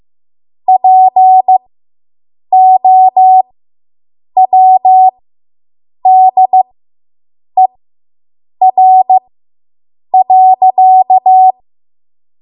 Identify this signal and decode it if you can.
Morse code, heard as a keyed tone, about 15 wpm, spaced as for 7 wpm, 750 Hz. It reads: POWDER.